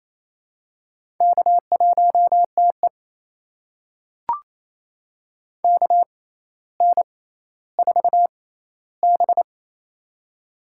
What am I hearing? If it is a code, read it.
Morse code, heard as a keyed tone, about 28 wpm, spaced as for 14 wpm, 700 Hz. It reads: K1TE KN4B
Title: USA Call Signs - 4 Characters - 6x Character Spacing - 28wpm